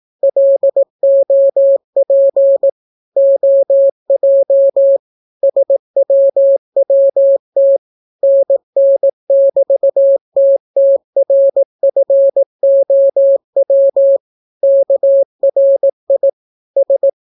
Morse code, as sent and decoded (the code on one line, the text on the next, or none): .-.. --- .--. / --- .--- / ... .-- .-- - / -. -. -...- - - .-. ..-. --- .-- / -.- .-. .. / ...
LOP OJ SWWT NN<BT>TTRFOW KRI S